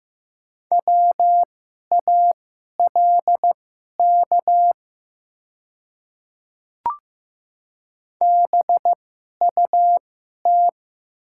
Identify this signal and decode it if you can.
Morse code, heard as a keyed tone, about 15 wpm, spaced as for 11 wpm, 690 Hz. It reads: WALK BUT
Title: 2x Character Spacing - Top 500 Words - 15wpm